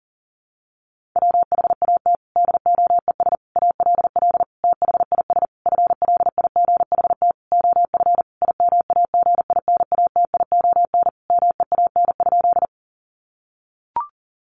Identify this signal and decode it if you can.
Morse code, heard as a keyed tone, about 40 wpm, 700 Hz. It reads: WHAT DOES ALL THIS FLIGHT OF IMAGINATION MEAN?
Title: Sentences from Top 1300 Words 40wpm